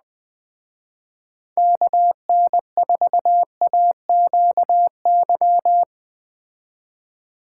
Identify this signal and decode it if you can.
Morse code, heard as a keyed tone, about 20 wpm, 700 Hz. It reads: KN4AQY
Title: Speed Racing - USA Call Signs 17wpm